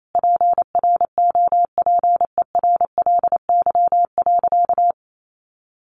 Morse code, heard as a keyed tone, about 28 wpm, 700 Hz. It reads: PROPERLY.
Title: Sentences from Top 1100 Words 28wpm